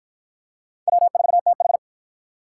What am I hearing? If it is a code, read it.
Morse code, heard as a keyed tone, about 48 wpm, 700 Hz. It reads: W4TH